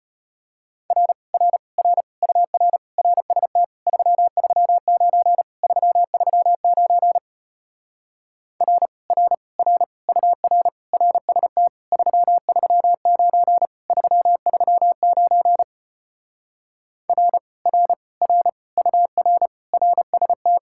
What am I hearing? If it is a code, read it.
Morse code, heard as a keyed tone, about 33 wpm, 700 Hz. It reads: R R R UR RST 339 339 R R R UR RST 339 339 R R R UR RST